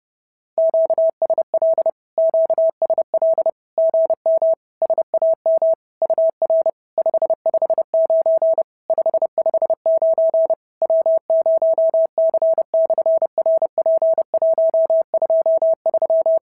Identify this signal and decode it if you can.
Morse code, heard as a keyed tone, about 30 wpm, 690 Hz. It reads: QSL QSL GM SAM UR 559 559 W0C/RP123
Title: SOTA QSO Practice 30wpm